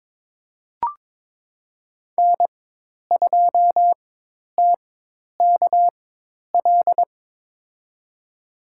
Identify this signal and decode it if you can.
Morse code, heard as a keyed tone, about 22 wpm, spaced as for 10 wpm, 700 Hz. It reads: N2TKL
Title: USA Call Signs - 4x Character Spacing - 22wpm